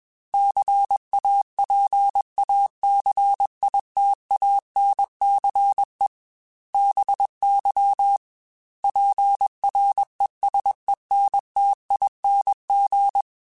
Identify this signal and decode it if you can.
Morse code, heard as a keyed tone, about 21 wpm, 800 Hz. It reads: CAPACITANCE BY PRESENTING
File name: Technical Subjects / Technical Subjects-mp3s / XC @20